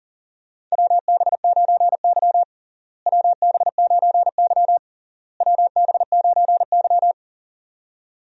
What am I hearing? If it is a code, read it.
Morse code, heard as a keyed tone, about 37 wpm, 700 Hz. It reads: WB9Y WB9Y WB9Y
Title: USA Call Signs - 4 Characters - Repeated 3x - 40wpm